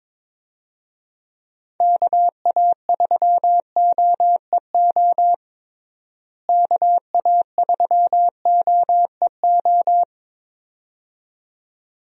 Morse code, heard as a keyed tone, about 22 wpm, 700 Hz. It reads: KA3OEO KA3OEO
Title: USA Call Signs - 6 Characters - Repeated 2x - 22wpm